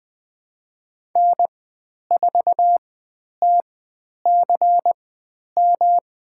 Morse code, heard as a keyed tone, about 20 wpm, spaced as for 10 wpm, 700 Hz. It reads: N4TCM
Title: 109 Lesson 18g - Letter m - Call Signs 20wpm